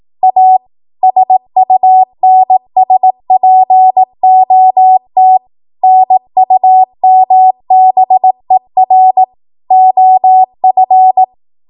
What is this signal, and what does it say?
Morse code, heard as a keyed tone, about 18 wpm, 750 Hz. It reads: A SUNSPOT NUMBER OF
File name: Archive / 18 WPM / 18WPM